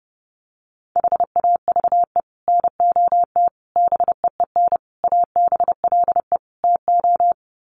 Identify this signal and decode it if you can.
Morse code, heard as a keyed tone, about 30 wpm, 700 Hz. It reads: HAVE NOT BEEN ABLE TO
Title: Sets of 5 Words 30wpm